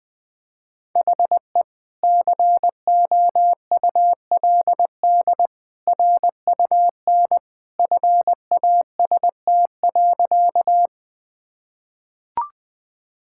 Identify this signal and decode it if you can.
Morse code, heard as a keyed tone, about 21 wpm, 700 Hz. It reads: HE COULD RUN FAST.
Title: Sentences with 2-Word Phrases - 20wpm